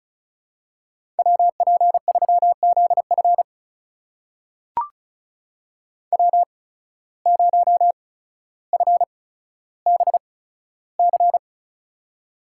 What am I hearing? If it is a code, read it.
Morse code, heard as a keyed tone, about 36 wpm, spaced as for 15 wpm, 700 Hz. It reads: WP3ZF W0FBC
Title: USA Call Signs - 5 Characters - 8x Character Spacing - 35wpm